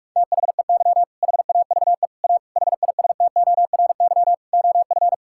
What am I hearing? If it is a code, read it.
Morse code, heard as a keyed tone, about 46 wpm, 700 Hz. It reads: THEY HAVE A HISTORY OF
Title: Sentences from Top 1000 Words - Review Part 5 - 45wpm